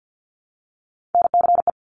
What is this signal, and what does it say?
Morse code, heard as a keyed tone, about 50 wpm, 700 Hz. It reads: NCE
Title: Top Three Letter Combinations in Words 50wpm